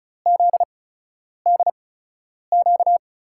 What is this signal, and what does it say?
Morse code, heard as a keyed tone, about 36 wpm, spaced as for 10 wpm, 700 Hz. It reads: ZDQ
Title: USA Call Signs - 6 Characters - 8x Character Spacing - 35wpm